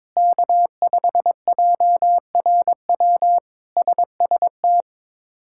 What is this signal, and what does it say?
Morse code, heard as a keyed tone, about 22 wpm, 700 Hz. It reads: K5JRW SST
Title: K1USN SST QSO Practice - 22wpm